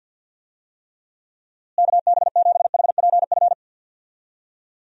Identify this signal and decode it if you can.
Morse code, heard as a keyed tone, about 50 wpm, 700 Hz. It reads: KB7HPF
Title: USA Call Signs - 6 Characters - Repeated 2x - 50wpm